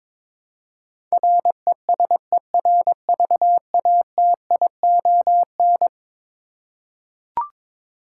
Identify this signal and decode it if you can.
Morse code, heard as a keyed tone, about 22 wpm, 700 Hz. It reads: RESERVATION
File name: practice / 22wpm / Eleven Letter Words - 22wpm